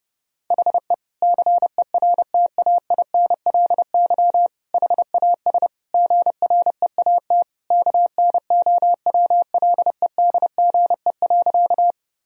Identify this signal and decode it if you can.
Morse code, heard as a keyed tone, about 28 wpm, 700 Hz. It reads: HE CERTAINLY HAS GREAT KNOWLEDGE.
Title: Sentences from Top 1000 Words - Review Part 3 - 30wpm